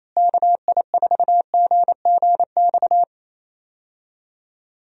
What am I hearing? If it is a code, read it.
Morse code, heard as a keyed tone, about 28 wpm, 700 Hz. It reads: KI4GGX
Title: USA Call Signs - 6 Characters - Repeated 3x - 28wpm